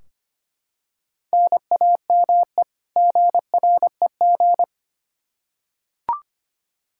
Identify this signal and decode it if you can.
Morse code, heard as a keyed tone, about 25 wpm, 700 Hz. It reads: NAME GREG
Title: Practice QSO - Exchange 1 - Beginning - 25wpm